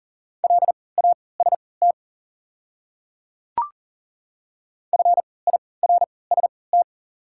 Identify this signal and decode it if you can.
Morse code, heard as a keyed tone, about 40 wpm, spaced as for 23 wpm, 700 Hz. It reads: LAST FIRST
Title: 3x Character Spacing - Top 300 Words - 40wpm